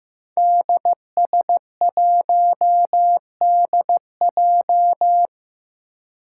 Morse code, heard as a keyed tone, about 15 wpm, 700 Hz. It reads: DS1DJ
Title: Five Challenging Letter-Number Combinations 15wpm